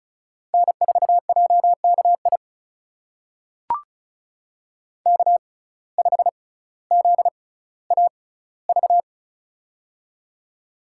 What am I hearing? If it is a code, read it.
Morse code, heard as a keyed tone, about 36 wpm, spaced as for 18 wpm, 700 Hz. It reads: N4JKI K5ZAV
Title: USA Call Signs - 5 Characters - 6x Character Spacing - 35wpm